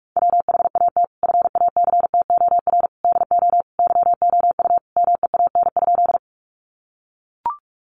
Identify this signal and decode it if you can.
Morse code, heard as a keyed tone, about 44 wpm, 700 Hz. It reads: WHAT FACTOR DO YOU MEAN?
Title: Sentences from Top 1300 Words 45wpm